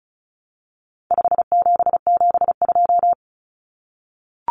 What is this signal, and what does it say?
Morse code, heard as a keyed tone, about 34 wpm, 700 Hz. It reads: 5772
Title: Common Cut Numbers - 4-Digit 35wpm